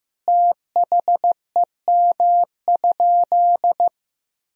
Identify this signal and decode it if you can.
Morse code, heard as a keyed tone, about 15 wpm, 700 Hz. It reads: THEM?
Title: Sentences from Top 1000 Words - Review Part 2 - 15wpm